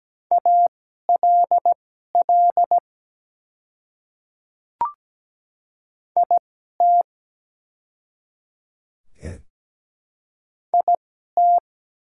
Morse code, heard as a keyed tone, about 17 wpm, spaced as for 12 wpm, 700 Hz. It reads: ALL IT IT
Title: 2x Character Spacing - Top 100 Words - 17wpm